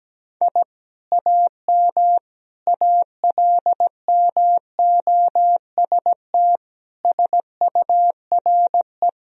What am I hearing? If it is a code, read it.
Morse code, heard as a keyed tone, about 17 wpm, 700 Hz. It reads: I AM ALMOST SURE
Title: Sentences from Top 1000 Words - Review Part 4 - 17wpm